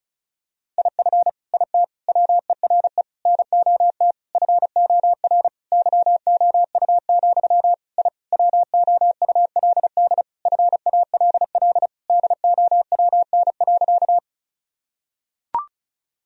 Morse code, heard as a keyed tone, about 34 wpm, 700 Hz. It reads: IF IT WERE NOT FOR YOU, I WOULD FALL DOWN.
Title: Sentences from Top 1000 Words - Review Part 2 - 35wpm